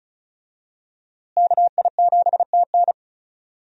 Morse code, heard as a keyed tone, about 36 wpm, 700 Hz. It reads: KI7TN
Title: USA Call Signs - 5 Characters - Repeated 2x - 35wpm